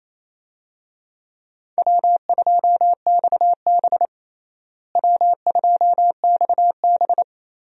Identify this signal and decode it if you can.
Morse code, heard as a keyed tone, about 28 wpm, 700 Hz. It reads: W2XB W2XB
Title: USA Call Signs - 4 Characters - Repeated 3x - 28wpm